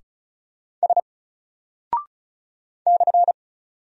Morse code, heard as a keyed tone, about 36 wpm, 700 Hz. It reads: S /
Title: Single Character (Letter:Number:Punctuation:Prosign) 35wpm